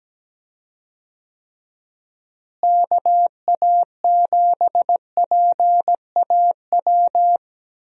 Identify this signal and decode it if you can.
Morse code, heard as a keyed tone, about 17 wpm, 700 Hz. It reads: KA7PAW
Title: USA Call Signs - 6 Characters - Repeated 3x - 17wpm